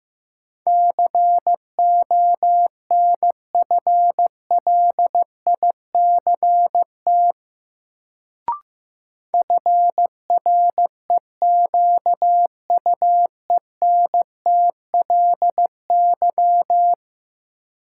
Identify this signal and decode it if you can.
Morse code, heard as a keyed tone, about 15 wpm, 700 Hz. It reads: CONFLICT FREQUENTLY
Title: Sentences from Top 1300 Words - New Word Drill - 15wpm